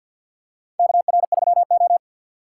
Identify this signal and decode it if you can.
Morse code, heard as a keyed tone, about 47 wpm, 700 Hz. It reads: KR3O
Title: USA Call Signs - 4 Characters - 3x Character Spacing - 50wpm